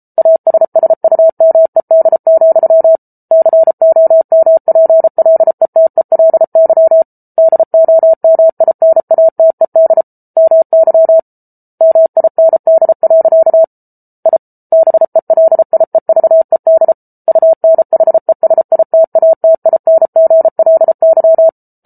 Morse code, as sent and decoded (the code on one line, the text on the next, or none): .- ... ... ..- -- . -.. --..-- / -.-. --- -- .--. .-.. - . .-.. -.-- / -.. --- -- .. -. .- - . -.. / -- -.-- / -- .. -. -.. .-.-.- / .. / -... .-.. .. ...- . -.. / ..- -. .... ... .. - .- - .. -. --. .-.. -.--
ASSUMED, COMPLETELY DOMINATED MY MIND. I BELIEVED UNHESITATINGLY